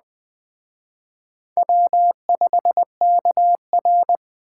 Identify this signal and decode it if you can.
Morse code, heard as a keyed tone, about 20 wpm, 700 Hz. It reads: W5KR
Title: Speed Racing - USA 1x2 Call Signs 17wpm